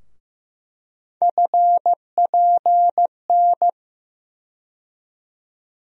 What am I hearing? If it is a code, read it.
Morse code, heard as a keyed tone, about 15 wpm, 700 Hz. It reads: FPN
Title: USA Call Sign Postfixes - 3 Letters 15wpm